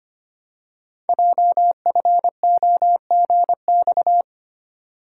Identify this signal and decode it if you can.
Morse code, heard as a keyed tone, about 25 wpm, 700 Hz. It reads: JFOGX
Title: Code Groups - 5 Letters - 25wpm